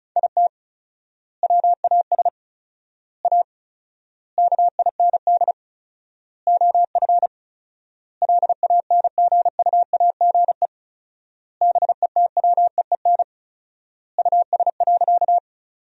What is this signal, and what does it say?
Morse code, heard as a keyed tone, about 36 wpm, 700 Hz. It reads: IT WAS A KIND OF LANGUAGE BETWEEN US.
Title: 4x Word-Spacing - Sentences from Top 500 Words - 35wpm